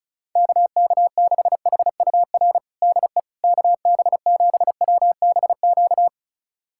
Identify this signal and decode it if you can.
Morse code, heard as a keyed tone, about 36 wpm, 700 Hz. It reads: KK6HUR DE KB7WBQ
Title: USA Callsign DE Callsign - 6 Characters - 35wpm